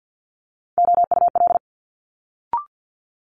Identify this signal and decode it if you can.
Morse code, heard as a keyed tone, about 50 wpm, 700 Hz. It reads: OUL
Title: Multi-letter Phonemes Quick Fire 50wpm